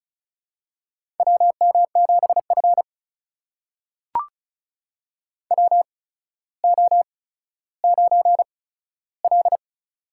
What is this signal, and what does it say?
Morse code, heard as a keyed tone, about 36 wpm, spaced as for 15 wpm, 700 Hz. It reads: WM7F WO9L
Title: USA Call Signs - 4 Characters - 8x Character Spacing - 35wpm